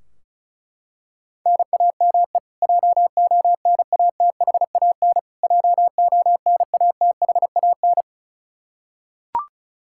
Morse code, heard as a keyed tone, about 34 wpm, 700 Hz. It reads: NAME JONATHAN JONATHAN
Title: Practice QSO - Exchange 1 - Beginning - 35wpm